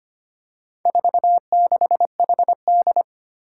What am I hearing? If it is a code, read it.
Morse code, heard as a keyed tone, about 25 wpm, 700 Hz. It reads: 46HD
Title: Four Challenging Letter-Number Combinations 25wpm